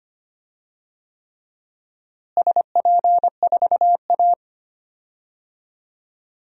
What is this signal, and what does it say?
Morse code, heard as a keyed tone, about 25 wpm, 700 Hz. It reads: SP4A
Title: DXCC-100 Call Signs - 4 Characters - Repeated 3x - 25wpm